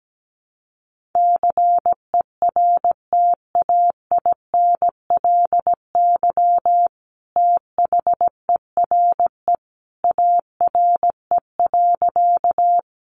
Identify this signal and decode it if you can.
Morse code, heard as a keyed tone, about 17 wpm, 700 Hz. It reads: CERTAINLY THERE ARE.
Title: Sentences from Top 900 Words 17wpm